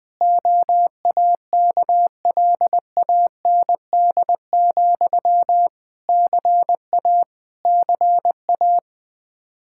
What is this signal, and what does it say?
Morse code, heard as a keyed tone, about 20 wpm, 700 Hz. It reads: OAKLAND, CA CA
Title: Practice QSO - Exchange 1 - Beginning - 20wpm